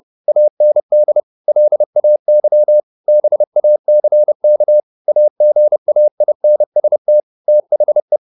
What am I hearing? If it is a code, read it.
Morse code, heard as a keyed tone, about 30 wpm, 600 Hz. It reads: AND LAY BACK AGAINST THE
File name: the old man and the sea / the old man and the sea 1 30wpm